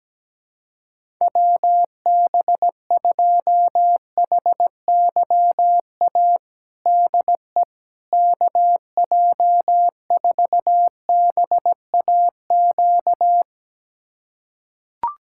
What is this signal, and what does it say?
Morse code, heard as a keyed tone, about 17 wpm, 700 Hz. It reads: WB2HYA DE KJ4BAQ
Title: USA Callsign DE Callsign - 6 Characters - 17wpm